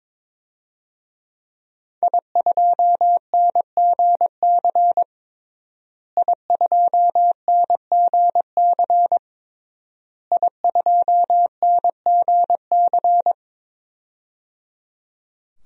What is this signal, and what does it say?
Morse code, heard as a keyed tone, about 22 wpm, 700 Hz. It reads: I2NGC I2NGC I2NGC
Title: DXCC-100 Call Signs - 5 Characters - Repeated 3x - 22wpm